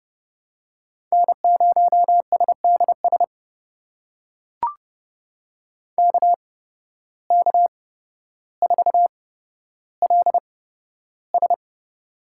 Morse code, heard as a keyed tone, about 30 wpm, spaced as for 13 wpm, 700 Hz. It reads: N0SDS KK4LS
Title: USA Call Signs - 5 Characters - 8x Character Spacing - 30wpm